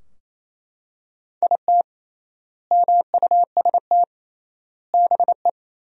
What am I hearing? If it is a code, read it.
Morse code, heard as a keyed tone, about 28 wpm, 700 Hz. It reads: IT MUST BE
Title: 3x Word-Spacing - Sentences from Top 500 Words - 28wpm